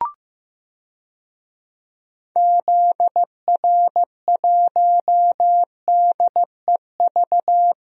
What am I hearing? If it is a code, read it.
Morse code, heard as a keyed tone, about 16 wpm, 700 Hz. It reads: ZR1DEV
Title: DXCC-100 Call Signs - 6 Characters - Repeated 3x - 15wpm